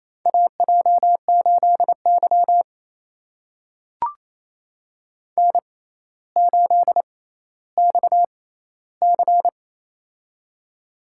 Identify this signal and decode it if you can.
Morse code, heard as a keyed tone, about 28 wpm, spaced as for 14 wpm, 700 Hz. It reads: AJ8Y N8XC
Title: USA Call Signs - 6x Character Spacing - 28wpm